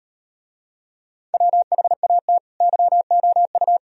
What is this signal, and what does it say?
Morse code, heard as a keyed tone, about 38 wpm, 700 Hz. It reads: WHAT YOU